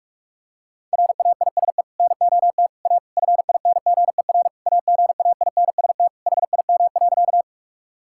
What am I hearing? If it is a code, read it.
Morse code, heard as a keyed tone, about 46 wpm, 700 Hz. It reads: RAISE NOT A FINGER AGAINST HIM.
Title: Sentences from Top 1600 Words 45wpm